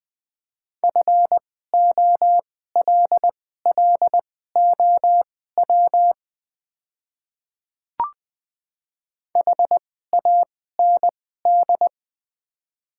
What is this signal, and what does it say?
Morse code, heard as a keyed tone, about 20 wpm, spaced as for 15 wpm, 700 Hz. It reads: FOLLOW HAND
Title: 2x Character Spacing - Top 200 Words - 20wpm